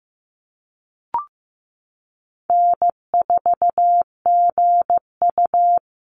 Morse code, heard as a keyed tone, about 15 wpm, 700 Hz. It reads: N4GU
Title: Easy Call Signs - 15wpm